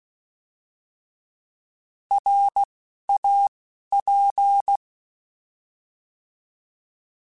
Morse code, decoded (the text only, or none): RAP